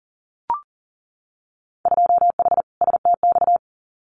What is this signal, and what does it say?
Morse code, heard as a keyed tone, about 42 wpm, 700 Hz. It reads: 2H STX
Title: Winter Field Day Exchanges Encore 40wpm